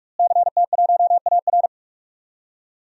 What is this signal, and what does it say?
Morse code, heard as a keyed tone, about 41 wpm, 700 Hz. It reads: CT1AR
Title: DXCC-100 Call Signs - 5 Characters - Repeated 3x - 45wpm